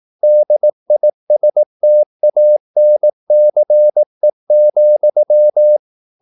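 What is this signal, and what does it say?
Morse code, heard as a keyed tone, about 18 wpm, 600 Hz. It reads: DISTANCE,
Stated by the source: Morse practice at 18 wpm, English text, from CW med Gnister.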